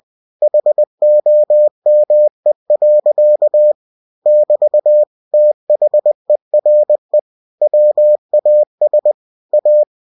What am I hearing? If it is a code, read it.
Morse code, heard as a keyed tone, about 20 wpm, 600 Hz. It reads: HOME. <BT> THERE WAS A